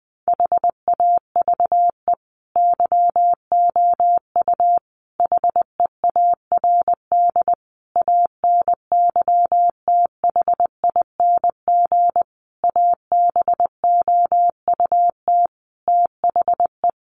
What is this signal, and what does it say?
Morse code, heard as a keyed tone, about 20 wpm, 700 Hz. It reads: HAVE YOU HEARD ANYTHING ABOUT THE
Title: Sentences from Top 1000 Words 20wpm